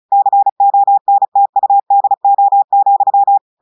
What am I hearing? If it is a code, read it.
Morse code, heard as a keyed tone, about 36 wpm, 800 Hz. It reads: CONTUDO,
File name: portugais texte en clair 2 vitesse 35 mots minutes